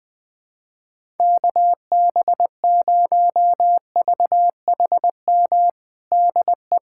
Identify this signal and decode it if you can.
Morse code, heard as a keyed tone, about 21 wpm, 700 Hz. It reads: KB0VHM DE
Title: USA Callsign DE Callsign - 6 Characters - 20wpm